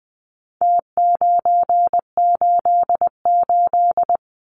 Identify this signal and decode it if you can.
Morse code, heard as a keyed tone, about 20 wpm, 700 Hz. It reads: T988
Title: Numbers - Contest Style - 4-Digit 20wpm